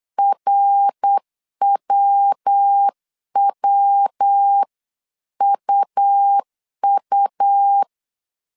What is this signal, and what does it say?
Morse code, heard as a keyed tone, about 8 wpm, 800 Hz. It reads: RWWUU
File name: CW / DET / 1 3 1500C S C 8wpm 4214